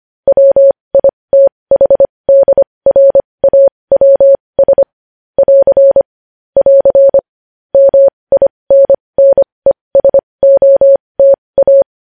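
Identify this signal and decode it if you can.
Morse code, heard as a keyed tone, about 25 wpm, spaced as for 20 wpm, 550 Hz. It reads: WITHDRAWS <AR> <AR> MINNESOTA
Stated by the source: Morse practice sending news headlines at 20 wpm.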